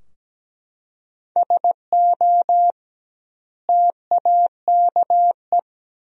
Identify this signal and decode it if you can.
Morse code, heard as a keyed tone, about 17 wpm, 700 Hz. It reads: SO TAKE
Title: 2x Word-Spacing - Sentences from Top 300 Words - 17wpm